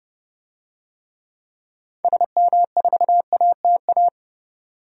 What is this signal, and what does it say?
Morse code, heard as a keyed tone, about 31 wpm, 700 Hz. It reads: SM4ATA
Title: DXCC-100 Call Signs - 6 Characters - 30wpm